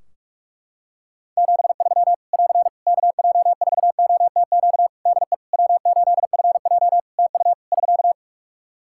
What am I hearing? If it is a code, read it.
Morse code, heard as a keyed tone, about 44 wpm, 700 Hz. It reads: 73 <AR> KJ4OTQ DE W8FJ TU <SK>
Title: Practice QSO - Exchange 4 - Ending - 45wpm